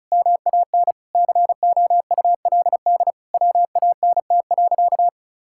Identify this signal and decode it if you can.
Morse code, heard as a keyed tone, about 36 wpm, 700 Hz. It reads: MAN COULD WANT.
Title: Sentences from Top 1000 Words - Review Part 5 - 35wpm